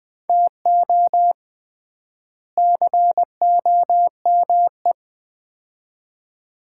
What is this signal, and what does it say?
Morse code, heard as a keyed tone, about 20 wpm, 700 Hz. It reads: TO COME
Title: 3x Word-Spacing - Sets of 2 Words - 20wpm